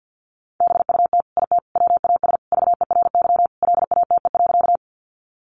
Morse code, heard as a keyed tone, about 50 wpm, 700 Hz. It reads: BUT IT WAS VERY LATE.
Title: Sentences from Top 300 Words 50wpm